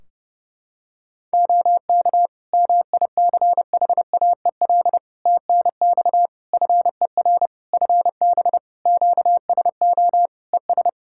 Morse code, heard as a keyed tone, about 30 wpm, 700 Hz. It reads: OK MICHAEL TNX FER FB QSO ES
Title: Practice QSO - Exchange 4 - Ending - 30wpm